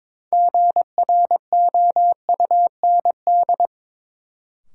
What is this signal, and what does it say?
Morse code, heard as a keyed tone, about 22 wpm, 700 Hz. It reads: GROUND
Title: Sets of 3 Words - New Word Drill - 22wpm